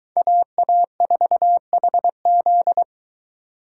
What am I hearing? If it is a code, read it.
Morse code, heard as a keyed tone, about 22 wpm, 700 Hz. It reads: AA4HZ